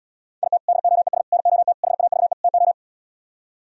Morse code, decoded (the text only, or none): IT DOES NOT HAVE TO